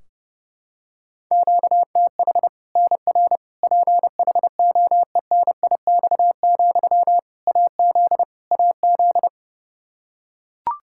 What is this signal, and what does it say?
Morse code, heard as a keyed tone, about 30 wpm, 700 Hz. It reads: QTH NR PHOENIX, AZ AZ
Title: K1USN SST QSO Practice 30wpm